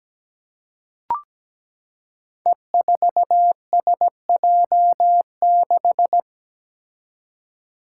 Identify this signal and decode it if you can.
Morse code, heard as a keyed tone, about 17 wpm, 700 Hz. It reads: E4SJ6
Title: Five Challenging Letter-Number Combinations 17wpm